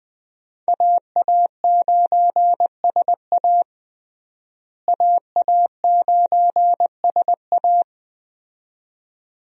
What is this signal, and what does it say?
Morse code, heard as a keyed tone, about 21 wpm, 700 Hz. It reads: AA9SA AA9SA
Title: USA Call Signs - 5 Characters - Repeated 2x - 20wpm